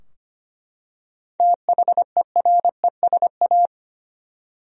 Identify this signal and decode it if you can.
Morse code, heard as a keyed tone, about 25 wpm, 700 Hz. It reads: THERESA
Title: USA Common Names - 25wpm